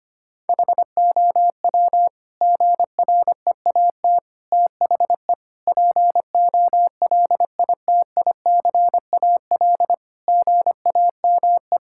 Morse code, heard as a keyed tone, about 25 wpm, 700 Hz. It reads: HOW GREAT THE POLITICAL GAME
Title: Sentences from Top 1000 Words - Review Part 5 - 25wpm